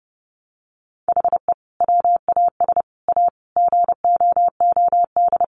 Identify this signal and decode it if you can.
Morse code, heard as a keyed tone, about 30 wpm, 700 Hz. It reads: HE WAS A GOOD
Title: Sentences from Top 1300 Words 30wpm